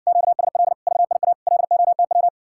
Morse code, heard as a keyed tone, about 60 wpm, 700 Hz. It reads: QSL VIA LOTW